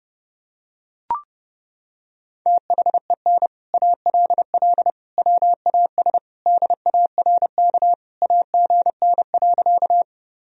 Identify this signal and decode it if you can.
Morse code, heard as a keyed tone, about 32 wpm, 700 Hz. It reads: THEN ALL WAS DARK AGN.
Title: Sentences from Top 500 Words and Abbreviations 30wpm